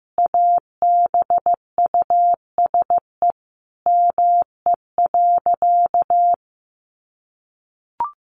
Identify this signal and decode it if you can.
Morse code, heard as a keyed tone, about 15 wpm, 700 Hz. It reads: ABUSE ME.
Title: Sentences from Top 1300 Words 15wpm